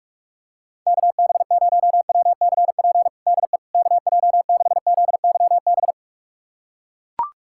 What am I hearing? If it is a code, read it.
Morse code, heard as a keyed tone, about 41 wpm, 700 Hz. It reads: KB0WCP DE KJ6ZYB
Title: USA Callsign DE Callsign - 6 Characters - 45wpm